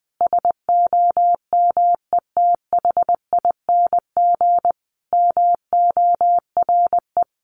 Morse code, heard as a keyed tone, about 20 wpm, 700 Hz. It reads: SOMETHING MORE
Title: Sentences from Top 600 Words 20wpm